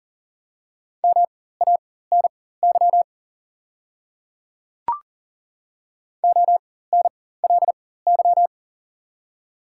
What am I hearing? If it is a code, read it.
Morse code, heard as a keyed tone, about 39 wpm, spaced as for 18 wpm, 690 Hz. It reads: MANY ONLY
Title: 4x Character Spacing - Top 500 Words - 40wpm